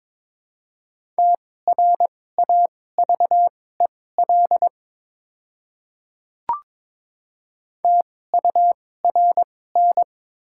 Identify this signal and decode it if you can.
Morse code, heard as a keyed tone, about 22 wpm, spaced as for 16 wpm, 700 Hz. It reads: TRAVEL TURN
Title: 2x Character Spacing - Top 500 Words - 22wpm